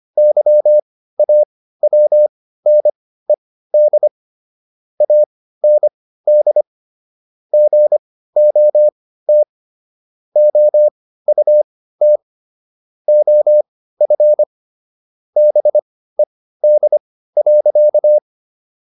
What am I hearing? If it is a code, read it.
Morse code, heard as a keyed tone, about 25 wpm, spaced as for 15 wpm, 600 Hz. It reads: YAWNED AND GOT OUT OF BED.